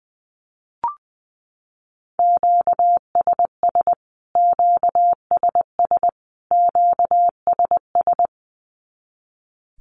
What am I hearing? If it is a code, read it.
Morse code, heard as a keyed tone, about 20 wpm, 700 Hz. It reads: QSS QSS QSS
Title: Q-Codes Repeated 3x 20wpm